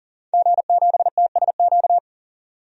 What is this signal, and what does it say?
Morse code, heard as a keyed tone, about 40 wpm, 700 Hz. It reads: G7TSQ
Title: DXCC-100 Call Signs - 5 Characters - 40wpm